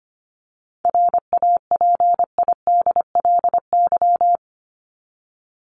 Morse code, heard as a keyed tone, about 25 wpm, 700 Hz. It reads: RAPIDLY
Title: LY Suffix - Top 100 Words with Base Words 25wpm